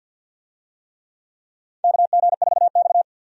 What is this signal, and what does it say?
Morse code, heard as a keyed tone, about 49 wpm, 700 Hz. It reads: KG4X
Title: USA Call Signs - 4 Characters - 50wpm